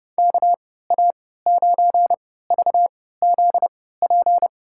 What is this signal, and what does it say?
Morse code, heard as a keyed tone, about 32 wpm, spaced as for 17 wpm, 700 Hz. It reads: KA9VZP
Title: USA Call Signs - 6 Characters - 3x Character Spacing - 30wpm